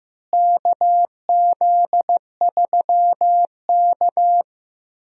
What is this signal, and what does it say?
Morse code, heard as a keyed tone, about 15 wpm, 700 Hz. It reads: KZ3K
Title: USA Call Signs - 4 Characters - Repeated 3x - 15wpm